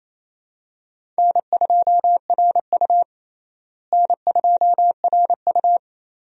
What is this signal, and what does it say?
Morse code, heard as a keyed tone, about 27 wpm, 700 Hz. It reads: N2RU N2RU
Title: USA Call Signs - 4 Characters - Repeated 3x - 28wpm